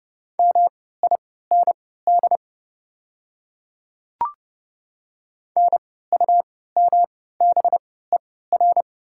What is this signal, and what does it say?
Morse code, heard as a keyed tone, about 30 wpm, spaced as for 17 wpm, 700 Hz. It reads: MIND NUMBER
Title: 3x Character Spacing - Top 500 Words - 30wpm